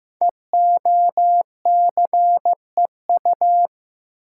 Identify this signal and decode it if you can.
Morse code, heard as a keyed tone, about 15 wpm, 700 Hz. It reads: EOCEU
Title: Code Groups - 5 Letter:Numbers - 15wpm